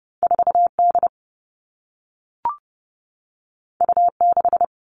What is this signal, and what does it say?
Morse code, heard as a keyed tone, about 30 wpm, 700 Hz. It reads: 4 D U 6
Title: Two Challenging Letter-Number Combinations 30wpm